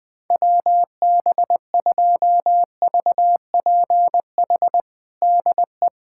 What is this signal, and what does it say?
Morse code, heard as a keyed tone, about 20 wpm, 700 Hz. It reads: WB2VPH DE
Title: USA Callsign DE Callsign 20wpm